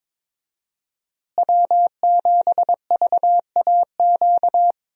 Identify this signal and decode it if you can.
Morse code, heard as a keyed tone, about 21 wpm, 700 Hz. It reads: W7VAQ
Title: USA Call Signs - 5 Characters - Repeated 2x - 22wpm